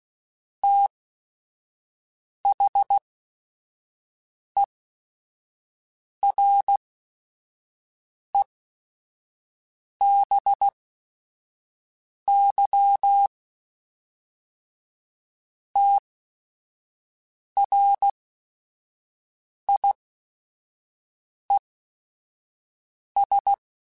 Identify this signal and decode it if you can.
Morse code, heard as a keyed tone, about 16 wpm, spaced as for 5 wpm, 800 Hz. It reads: THEREBY TRIES
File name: Technical Subjects / Technical Subjects-mp3s / XC @5